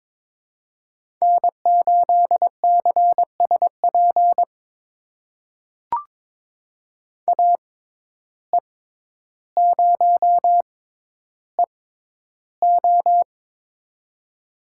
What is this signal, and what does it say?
Morse code, heard as a keyed tone, about 21 wpm, spaced as for 11 wpm, 700 Hz. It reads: N8CSP AE0EO
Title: USA Call Signs - 5 Characters - 6x Character Spacing - 22wpm